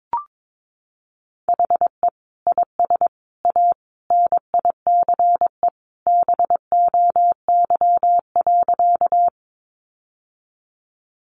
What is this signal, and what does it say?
Morse code, heard as a keyed tone, about 22 wpm, 700 Hz. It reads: HE IS A NICE BOY.
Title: Sentences from Top 1100 Words 22wpm